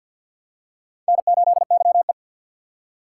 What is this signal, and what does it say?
Morse code, heard as a keyed tone, about 49 wpm, 700 Hz. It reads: N8YE
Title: USA Call Signs - 4 Characters - 3x Character Spacing - 50wpm